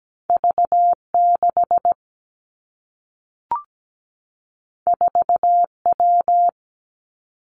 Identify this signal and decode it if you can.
Morse code, heard as a keyed tone, about 17 wpm, 700 Hz. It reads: V6 4W
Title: Two Challenging Letter-Number Combinations 17wpm